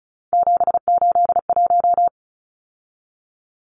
Morse code, heard as a keyed tone, about 34 wpm, 700 Hz. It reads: 781
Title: Numbers - 3-Digit 35wpm